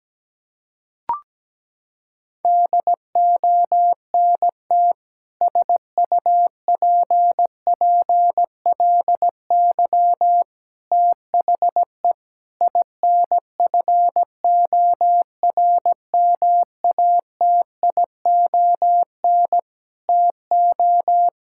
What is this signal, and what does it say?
Morse code, heard as a keyed tone, about 17 wpm, 700 Hz. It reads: DONT SUPPLY THE INFORMATION TO
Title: Sentences from Top 1000 Words - Review Part 5 - 17wpm